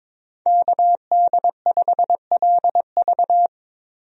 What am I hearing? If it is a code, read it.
Morse code, heard as a keyed tone, about 22 wpm, 700 Hz. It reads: KD5LV
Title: USA Call Signs - 5 Characters - 22wpm